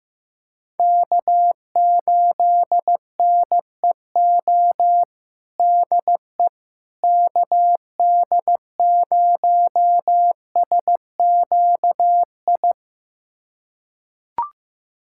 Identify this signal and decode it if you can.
Morse code, heard as a keyed tone, about 15 wpm, 700 Hz. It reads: K8NEO DE KD0SQI
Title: USA Callsign DE Callsign 15wpm